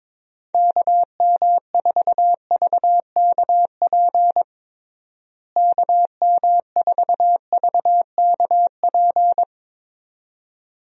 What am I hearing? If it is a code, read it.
Morse code, heard as a keyed tone, about 22 wpm, 700 Hz. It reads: KM4VKP KM4VKP
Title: USA Call Signs - 6 Characters - Repeated 2x - 22wpm